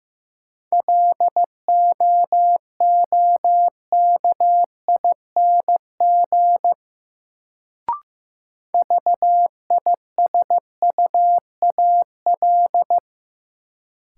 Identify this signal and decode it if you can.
Morse code, heard as a keyed tone, about 15 wpm, 700 Hz. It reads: LOOKING VISUAL
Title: Sentences from Top 1000 Words - Review Part 6 - New Word Drill - 15wpm